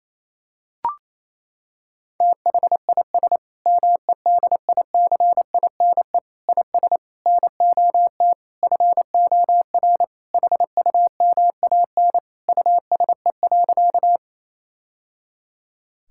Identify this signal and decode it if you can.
Morse code, heard as a keyed tone, about 28 wpm, 700 Hz. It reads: THIS MEDICINE IS NOT FOR HUMAN USE.
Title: Sentences from Top 1000 Words - Review Part 6 - 28wpm